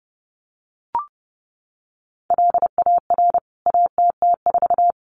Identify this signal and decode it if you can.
Morse code, heard as a keyed tone, about 29 wpm, 700 Hz. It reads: LAR ATT4
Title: CWT Exchange 30wpm